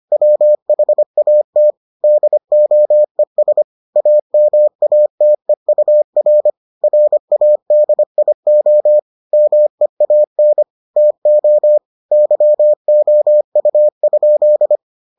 Morse code, heard as a keyed tone, about 25 wpm, 600 Hz. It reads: WHAT DOES AMATEUR RADIO MEAN TO YOU?